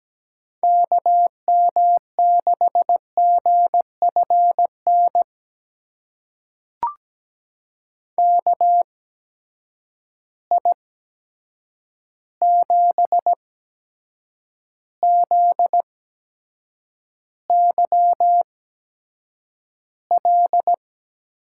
Morse code, heard as a keyed tone, about 17 wpm, spaced as for 7 wpm, 700 Hz. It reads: KM6GFN KI7ZYL
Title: USA Call Signs - 6 Characters - 8x Character Spacing - 17wpm